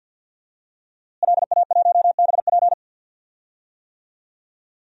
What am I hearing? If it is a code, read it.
Morse code, heard as a keyed tone, about 52 wpm, 700 Hz. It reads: LA1BP
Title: DXCC-100 Call Signs - 5 Characters - Repeated 3x - 50wpm